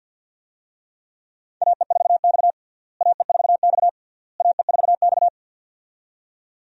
Morse code, heard as a keyed tone, about 52 wpm, 700 Hz. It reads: AE4X AE4X AE4X
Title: USA Call Signs - 4 Characters - Repeated 3x - 50wpm